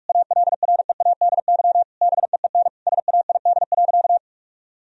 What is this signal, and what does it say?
Morse code, heard as a keyed tone, about 42 wpm, 700 Hz. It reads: ALREADY BEEN SAID.
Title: Sentences from Top 1000 Words - Review Part 1 - 45wpm